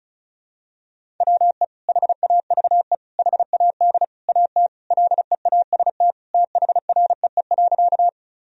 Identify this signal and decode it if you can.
Morse code, heard as a keyed tone, about 36 wpm, 700 Hz. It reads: WE HAVE HAD AT LEAST THREE.
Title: Sentences from Top 1000 Words - Review Part 1 - 35wpm